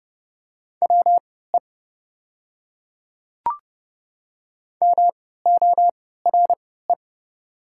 Morse code, heard as a keyed tone, about 30 wpm, spaced as for 17 wpm, 700 Hz. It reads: WE MORE
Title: 3x Character Spacing - Top 100 Words - 30wpm